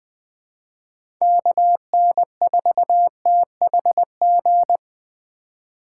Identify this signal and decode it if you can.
Morse code, heard as a keyed tone, about 20 wpm, 700 Hz. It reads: KN4THG
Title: USA Call Signs - 6 Characters - 3x Character Spacing - 20wpm